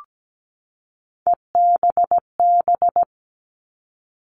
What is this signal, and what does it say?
Morse code, heard as a keyed tone, about 17 wpm, 700 Hz. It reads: EBB
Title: All Cut Numbers - 3-Digit 17wpm